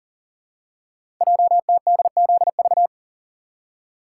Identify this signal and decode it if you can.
Morse code, heard as a keyed tone, about 43 wpm, 700 Hz. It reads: JTDZV
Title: Code Groups - 5 Letters - 40wpm